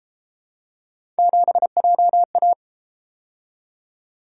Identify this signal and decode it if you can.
Morse code, heard as a keyed tone, about 30 wpm, 700 Hz. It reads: 7JA